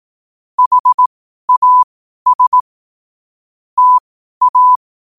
Morse code, heard as a keyed tone, about 18 wpm, spaced as for 12 wpm, 1000 Hz. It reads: HAS TA
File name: qotdfiles / 12wpm